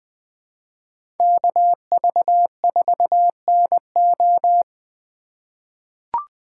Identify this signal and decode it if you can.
Morse code, heard as a keyed tone, about 20 wpm, 700 Hz. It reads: KV4NO
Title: USA Call Signs - 5 Characters - 2x Character Spacing - 20wpm